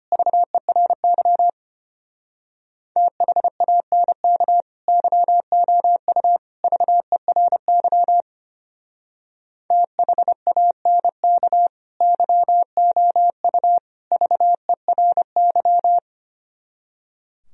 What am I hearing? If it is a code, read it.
Morse code, heard as a keyed tone, about 31 wpm, 700 Hz. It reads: VERY THANK YOU VERY THANK YOU VERY
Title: Speed-Racing - Sets of 3 Words 25wpm